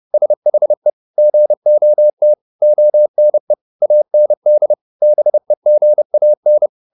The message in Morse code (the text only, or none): SHE GOT ONE AND BEGAN